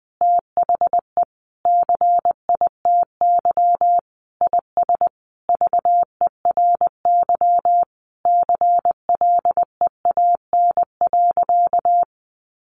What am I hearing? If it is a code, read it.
Morse code, heard as a keyed tone, about 19 wpm, 700 Hz. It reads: THE CITY IS VERY CLEAN.
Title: Easy Phrases 20wpm